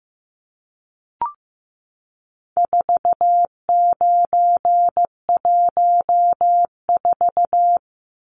Morse code, heard as a keyed tone, about 15 wpm, 700 Hz. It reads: 4914
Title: Numbers - 4-Digit 15wpm